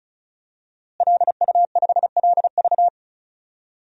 Morse code, heard as a keyed tone, about 34 wpm, 700 Hz. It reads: LU5LV
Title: DXCC-100 Call Signs - 5 Characters - Repeated 3x - 35wpm